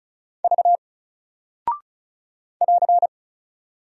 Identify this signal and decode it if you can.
Morse code, heard as a keyed tone, about 36 wpm, 700 Hz. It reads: V <AR>
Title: Single Character (Letter:Number:Punctuation:Prosign) 35wpm